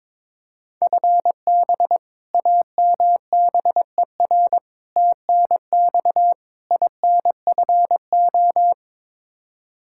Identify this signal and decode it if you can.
Morse code, heard as a keyed tone, about 22 wpm, 700 Hz. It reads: FB AMBER TNX INFO
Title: Practice QSO - Exchange 3 - More Info - 22wpm